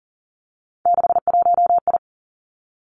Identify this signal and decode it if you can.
Morse code, heard as a keyed tone, about 39 wpm, 700 Hz. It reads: 61I
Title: Three Challenging Letter-Number Combinations 40wpm